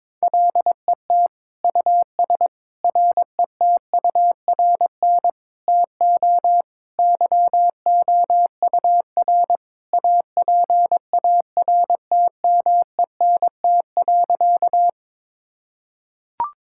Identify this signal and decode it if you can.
Morse code, heard as a keyed tone, about 22 wpm, 700 Hz. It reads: LET US RETURN TO YOUR APARTMENT.
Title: Sentences from Top 1000 Words - Review Part 6 - 22wpm